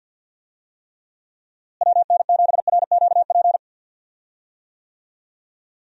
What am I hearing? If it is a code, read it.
Morse code, heard as a keyed tone, about 49 wpm, 700 Hz. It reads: WN7RQP
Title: USA Call Signs - 6 Characters - Repeated 2x - 50wpm